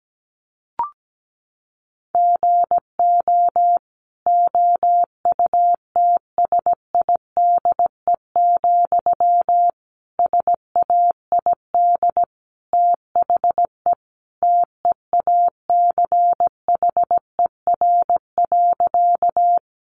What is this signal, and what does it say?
Morse code, heard as a keyed tone, about 17 wpm, 700 Hz. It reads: GO OUTSIDE, SAID THE TEACHER.
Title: Sentences from Top 900 Words 17wpm